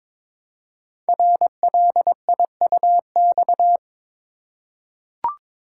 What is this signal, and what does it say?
Morse code, heard as a keyed tone, about 22 wpm, 700 Hz. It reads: RLIUX
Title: Code Groups - 5 Letter:Numbers - 22wpm